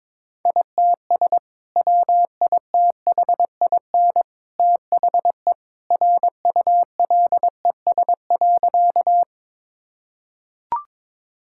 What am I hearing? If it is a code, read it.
Morse code, heard as a keyed tone, about 22 wpm, 700 Hz. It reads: ITS WITHIN THE RULES.
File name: practice / 22wpm / Sentences with 2-Word Phrases - 22wpm